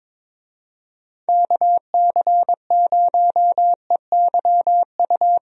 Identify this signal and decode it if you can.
Morse code, heard as a keyed tone, about 22 wpm, 700 Hz. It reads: KC0EYU
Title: USA Call Signs - 6 Characters - Repeated 3x - 22wpm